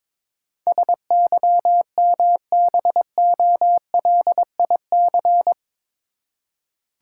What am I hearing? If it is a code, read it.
Morse code, heard as a keyed tone, about 22 wpm, 700 Hz. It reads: SYMBOLIC
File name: practice / 22wpm / Eight Letter Words - 22wpm